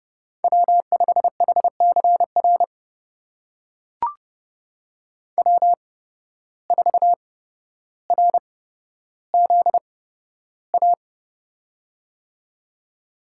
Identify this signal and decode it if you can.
Morse code, heard as a keyed tone, about 32 wpm, spaced as for 13 wpm, 700 Hz. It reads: W5HCR W4RZA
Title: USA Call Signs - 5 Characters - 8x Character Spacing - 30wpm